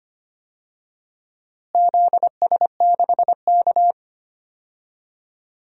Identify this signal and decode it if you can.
Morse code, heard as a keyed tone, about 25 wpm, 700 Hz. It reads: ZS6K
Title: DXCC-100 Call Signs - 4 Characters - Repeated 2x - 25wpm